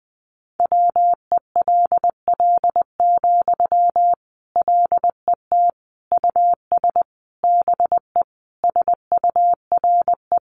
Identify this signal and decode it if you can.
Morse code, heard as a keyed tone, about 20 wpm, 700 Hz. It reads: WELL, LET US BE SURE
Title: Sentences from Top 300 Words 20wpm